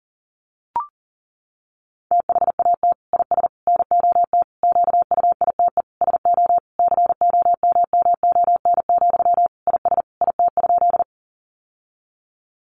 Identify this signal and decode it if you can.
Morse code, heard as a keyed tone, about 40 wpm, 700 Hz. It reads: THAT IS NOT QUITE SO COMMON, IS IT?
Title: Sentences from Top 700 Words 40wpm